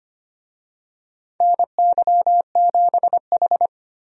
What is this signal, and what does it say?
Morse code, heard as a keyed tone, about 25 wpm, 700 Hz. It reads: NY7H
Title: USA Call Signs - 4 Characters - Repeated 2x - 25wpm